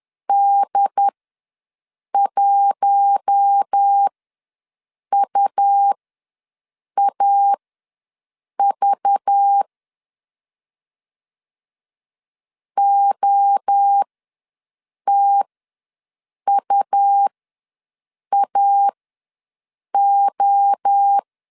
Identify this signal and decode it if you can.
Morse code, heard as a keyed tone, about 10 wpm, spaced as for 6 wpm, 800 Hz. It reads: D1UAV OTUAO
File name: LEZIONI / 5 LBVC / 5 LBV ws 10wpm 3645